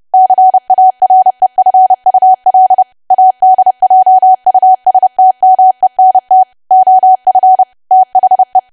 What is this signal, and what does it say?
Morse code, heard as a keyed tone, about 30 wpm, 750 Hz. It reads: CAREFUL ADJUSTMENT OF THE
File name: cwa / W1AW-Level3 / 30WPM